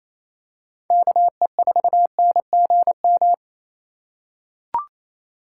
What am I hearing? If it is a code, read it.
Morse code, heard as a keyed tone, about 28 wpm, 700 Hz. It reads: KE4NGM
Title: USA Call Signs - 6 Characters - 2x Character Spacing - 28wpm